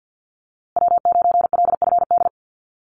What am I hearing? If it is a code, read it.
Morse code, heard as a keyed tone, about 50 wpm, 700 Hz. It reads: W9LFD
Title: USA Call Signs Repeated 2x 50wpm